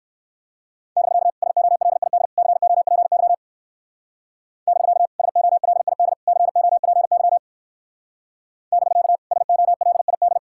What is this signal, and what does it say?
Morse code, heard as a keyed tone, about 58 wpm, 700 Hz. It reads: <BK> SOLID COPY <BK> SOLID COPY <BK> SOLID